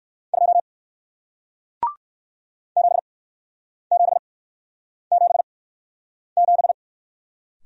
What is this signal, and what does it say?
Morse code, heard as a keyed tone, about 57 wpm, spaced as for 9 wpm, 700 Hz. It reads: 2 7777